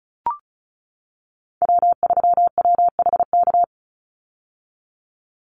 Morse code, heard as a keyed tone, about 34 wpm, 700 Hz. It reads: W3WHK
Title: Call Signs - 35wpm